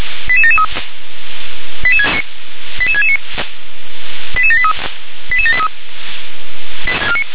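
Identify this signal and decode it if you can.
Five-tone selective call (ZVEI), heard as quick runs of five tones, tones of about 70 ms. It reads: 88773 88669 88783 88783
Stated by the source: ZVEI five-tone alert received by SDR.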